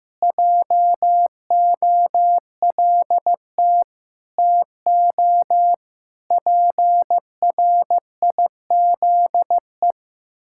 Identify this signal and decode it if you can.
Morse code, heard as a keyed tone, about 16 wpm, 700 Hz. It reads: JOLT TO PRIZE
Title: Pangram Challenge v2 - 15wpm